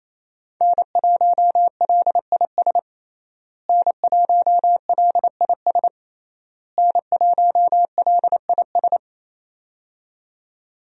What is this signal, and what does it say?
Morse code, heard as a keyed tone, about 28 wpm, 700 Hz. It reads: N1LIS N1LIS N1LIS
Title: USA Call Signs - 5 Characters - Repeated 3x - 28wpm